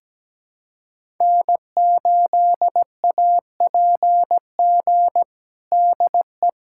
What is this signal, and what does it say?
Morse code, heard as a keyed tone, about 17 wpm, 700 Hz. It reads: N8APG DE
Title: USA Callsign DE Callsign - 5 Characters - 17wpm